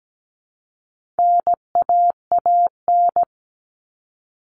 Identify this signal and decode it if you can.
Morse code, heard as a keyed tone, about 17 wpm, 700 Hz. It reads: NAAN
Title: All Cut Numbers - 4-Digit 17wpm